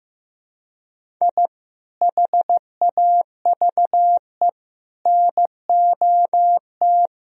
Morse code, heard as a keyed tone, about 15 wpm, 700 Hz. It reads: I HAVE NOT
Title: Sentences from Top 1000 Words - Review Part 1 - 15wpm